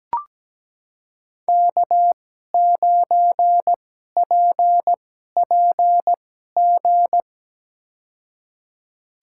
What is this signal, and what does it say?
Morse code, heard as a keyed tone, about 17 wpm, spaced as for 12 wpm, 700 Hz. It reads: K9PPG
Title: USA Call Signs - 5 Characters - 2x Character Spacing - 17wpm